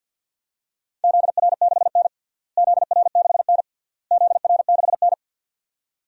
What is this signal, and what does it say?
Morse code, heard as a keyed tone, about 53 wpm, 700 Hz. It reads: ZR6N ZR6N ZR6N
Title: DXCC-100 Call Signs - 4 Characters - Repeated 3x - 50wpm